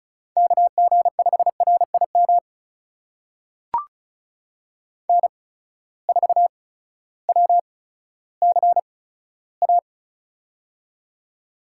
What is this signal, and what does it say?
Morse code, heard as a keyed tone, about 36 wpm, spaced as for 16 wpm, 700 Hz. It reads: KG5RIM N4WCA
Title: USA Call Signs - 8x Character Spacing - 35wpm